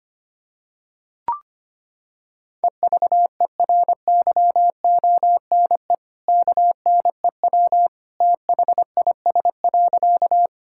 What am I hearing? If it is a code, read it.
Morse code, heard as a keyed tone, about 25 wpm, 700 Hz. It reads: EVERYONE KNEW THIS.
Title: Sentences from Top 1000 Words - Review Part 5 - 25wpm